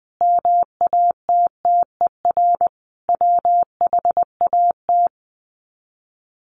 Morse code, heard as a keyed tone, about 20 wpm, 700 Hz. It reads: MATTER WHAT
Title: Sets of 5 Words Encore 20wpm